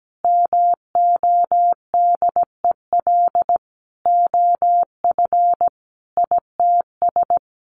Morse code, heard as a keyed tone, about 17 wpm, 700 Hz. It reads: MODEL OF ITS
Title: Sentences from Top 1000 Words 17wpm